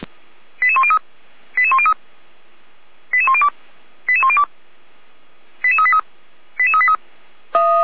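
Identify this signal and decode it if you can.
Five-tone selective call (ZVEI), heard as quick runs of five tones, tones of about 70 ms. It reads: 88103 88103 88102 88102 88383 88383
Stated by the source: ZVEI five-tone call 88149 received on an SDR, from the feuerwehr-orlamuende.org fire brigade audio archive.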